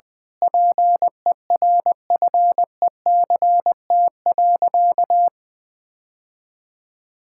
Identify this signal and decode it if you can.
Morse code, heard as a keyed tone, about 21 wpm, 700 Hz. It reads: PERFECT.
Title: Sentences with 8-Word Phrases - 20wpm